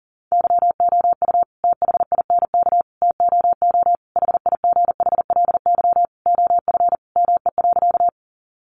Morse code, heard as a keyed tone, about 40 wpm, 700 Hz. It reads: YOU THINK TOO HIGHLY OF ME.
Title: Sentences from Top 1100 Words 40wpm